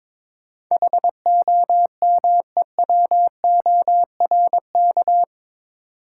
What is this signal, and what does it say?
Morse code, heard as a keyed tone, about 22 wpm, 700 Hz. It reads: HOMEWORK
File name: practice / 22wpm / Eight Letter Words - 22wpm